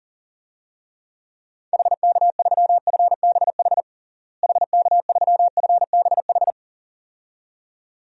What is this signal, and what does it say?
Morse code, heard as a keyed tone, about 37 wpm, 700 Hz. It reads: HK3FBH HK3FBH
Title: DXCC-100 Call Signs - 6 Characters - Repeated 2x - 40wpm